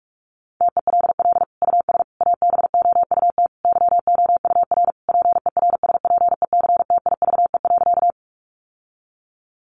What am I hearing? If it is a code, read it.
Morse code, heard as a keyed tone, about 44 wpm, 700 Hz. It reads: TELL US ABOUT YOUR PERSPECTIVE.
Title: Sentences from Top 1200 Words 45wpm